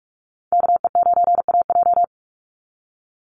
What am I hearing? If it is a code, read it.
Morse code, heard as a keyed tone, about 44 wpm, 700 Hz. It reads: KE9AJ
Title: NA SOTA Chaser Callsigns 45wpm